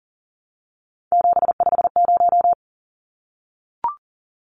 Morse code, heard as a keyed tone, about 40 wpm, 700 Hz. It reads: 750
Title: Numbers - 3-Digit 40wpm